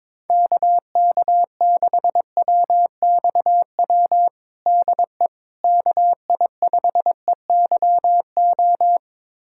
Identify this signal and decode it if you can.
Morse code, heard as a keyed tone, about 22 wpm, 700 Hz. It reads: KK6WXW DE KI5EYO
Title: USA Callsign DE Callsign - 6 Characters - 22wpm